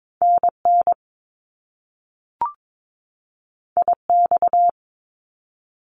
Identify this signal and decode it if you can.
Morse code, heard as a keyed tone, about 22 wpm, 700 Hz. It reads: NN IX
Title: Top Two Letter Combinations in Words 22wpm